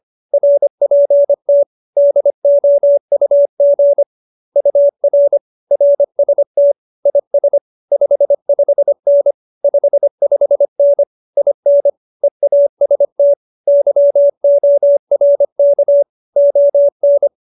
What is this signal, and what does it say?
Morse code, heard as a keyed tone, about 25 wpm, 570 Hz. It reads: RPT DOUG UR RST IS 55N 55N IN EAST YORK ON
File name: QSO220 25